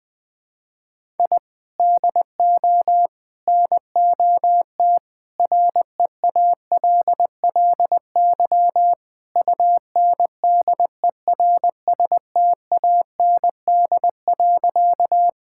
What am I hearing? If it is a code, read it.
Morse code, heard as a keyed tone, about 20 wpm, 700 Hz. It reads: I DO NOT REALLY UNDERSTAND.
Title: Sentences with 4-Word Phrases - Spoken Beginning - 20wpm